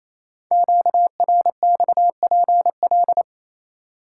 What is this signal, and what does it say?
Morse code, heard as a keyed tone, about 27 wpm, 700 Hz. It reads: QRXPL
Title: Code Groups - 5 Letters - 28wpm